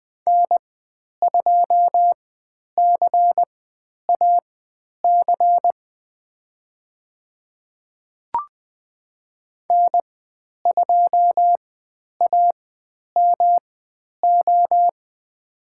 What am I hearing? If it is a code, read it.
Morse code, heard as a keyed tone, about 20 wpm, spaced as for 10 wpm, 700 Hz. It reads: N2CAC N2AMO
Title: 109 Lesson 18g - Letter m - Call Signs 20wpm